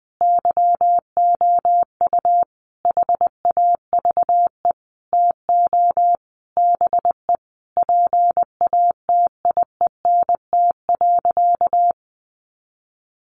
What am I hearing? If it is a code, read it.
Morse code, heard as a keyed tone, about 20 wpm, 700 Hz. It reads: YOU HAVE TO BE PATIENT.
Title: Easy Phrases 20wpm